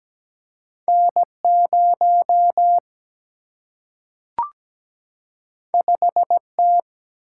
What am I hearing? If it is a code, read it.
Morse code, heard as a keyed tone, about 17 wpm, 700 Hz. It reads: N 0 5 T
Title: Single-Letter Single-Number 17wpm